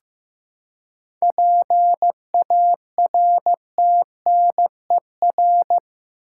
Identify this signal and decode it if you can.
Morse code, heard as a keyed tone, about 15 wpm, 700 Hz. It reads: PARTNER
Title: Seven Letter Words - 15wpm